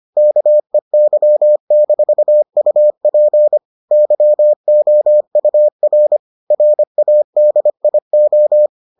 Morse code, heard as a keyed tone, about 25 wpm, 600 Hz. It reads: KEY-UP YOUR RADIO